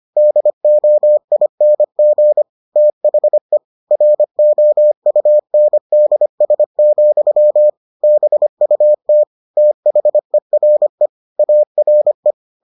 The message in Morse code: DOING THE ROUNDS, BUT THERE ARE

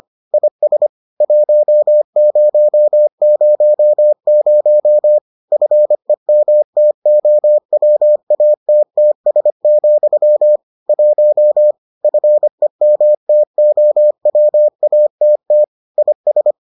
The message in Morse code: IS 1000 FEMTOWATTS, 1 FEMTOWATT IS

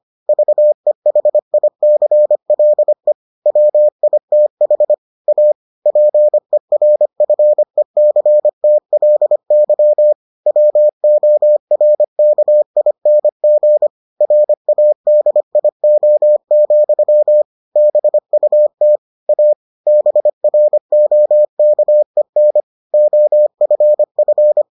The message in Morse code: VEHICLE WITH A PERFECTLY WORKING RADIO, BUT A BROKEN OFF

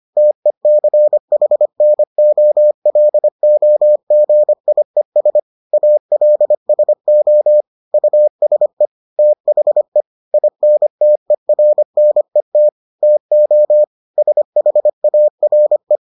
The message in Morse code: TECHNOLOGIES ALSO USE THE INTERNET TO SHARE

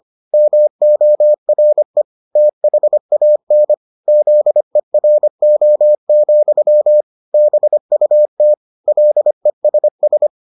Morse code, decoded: MORE THAN ZERO, BUT LESS

